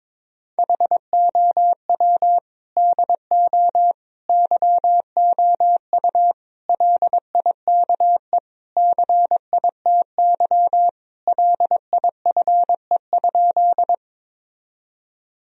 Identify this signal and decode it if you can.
Morse code, heard as a keyed tone, about 22 wpm, 700 Hz. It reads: HOW DO YOU LIKE CITY LIFE?
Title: Sentences from Top 1000 Words - Review Part 1 - 22wpm